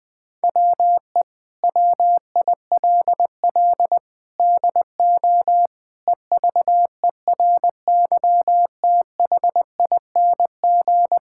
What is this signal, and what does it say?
Morse code, heard as a keyed tone, about 20 wpm, 700 Hz. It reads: WE WILL DO EVERYTHING
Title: Sentences with 7-Word Phrases - Spoken Beginning - 20wpm